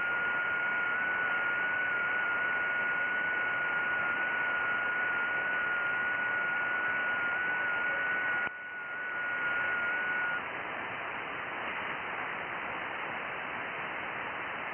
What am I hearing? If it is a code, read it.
FT8.
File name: IC-7100